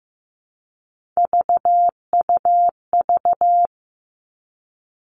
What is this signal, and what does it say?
Morse code, heard as a keyed tone, about 15 wpm, 700 Hz. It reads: VUV
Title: Three Challenging Letter-Number Combinations 15wpm